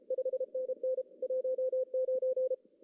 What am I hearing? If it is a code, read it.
Morse code, heard as a keyed tone, about 32 wpm, 520 Hz. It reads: 5NN 19
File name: CW / cqww-cw-k5tr